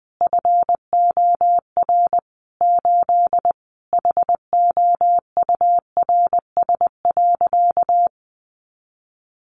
Morse code, heard as a keyed tone, about 20 wpm, 700 Hz. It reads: FOR 8 HOURS.
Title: Additional Easy Phrases 20wpm